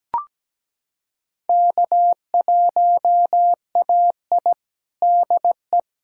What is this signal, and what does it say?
Morse code, heard as a keyed tone, about 17 wpm, 700 Hz. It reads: K1AI DE
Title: USA Callsign DE Callsign - 4 Characters - 17wpm